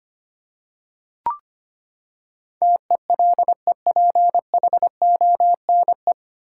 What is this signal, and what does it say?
Morse code, heard as a keyed tone, about 25 wpm, 700 Hz. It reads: TELEPHONE
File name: Nine Letter Words - 25wpm